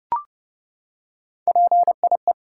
Morse code, heard as a keyed tone, about 30 wpm, 700 Hz. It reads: PIE
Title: Three Letter Words - 30wpm